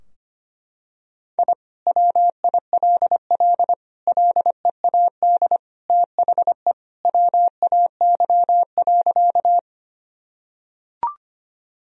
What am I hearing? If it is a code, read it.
Morse code, heard as a keyed tone, about 25 wpm, 700 Hz. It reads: I WILL LEAD THE WAY.
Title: Sentences from Top 1000 Words - Review Part 2 - 25wpm